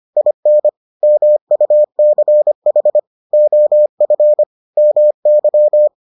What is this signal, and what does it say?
Morse code, heard as a keyed tone, about 25 wpm, 600 Hz. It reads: IN MUCH OF MY